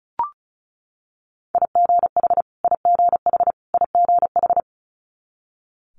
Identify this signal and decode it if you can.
Morse code, heard as a keyed tone, about 36 wpm, 700 Hz. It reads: IGH IGH IGH
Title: Multi-letter Phonemes Repeated 3x 35wpm